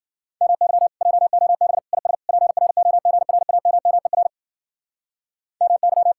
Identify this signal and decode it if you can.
Morse code, heard as a keyed tone, about 51 wpm, 700 Hz. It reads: MY JOB IS PROGRAMMER MY